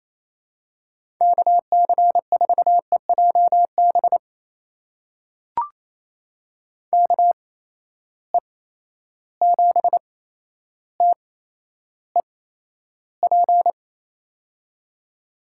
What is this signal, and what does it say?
Morse code, heard as a keyed tone, about 28 wpm, spaced as for 12 wpm, 700 Hz. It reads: KC4EJB KE7TEP
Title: USA Call Signs - 8x Character Spacing - 28wpm